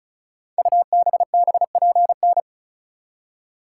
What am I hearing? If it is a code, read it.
Morse code, heard as a keyed tone, about 36 wpm, 690 Hz. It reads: UBBPN
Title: Code Groups - 5 Letter:Numbers - 35wpm